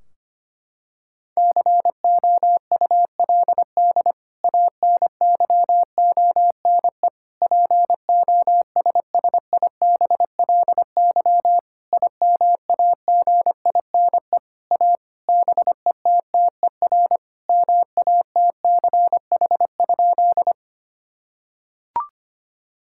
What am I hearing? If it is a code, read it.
Morse code, heard as a keyed tone, about 25 wpm, 700 Hz. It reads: COULD ANYONE POSSIBLY IMAGINE A BETTER MATCH?
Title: Sentences from Top 1600 Words 25wpm